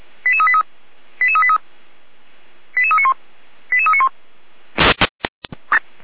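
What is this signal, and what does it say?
Five-tone selective call (ZVEI), heard as quick runs of five tones, tones of about 70 ms. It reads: 88383 88383 88391 88391
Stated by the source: Received by SDR, audio.f85360.feuerwehr-orlamuende.org fire brigade server.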